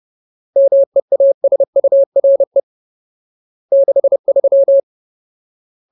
Morse code, heard as a keyed tone, about 30 wpm, 550 Hz. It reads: MEASURE 63
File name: Ultimate Word List / 30WPM Ultimate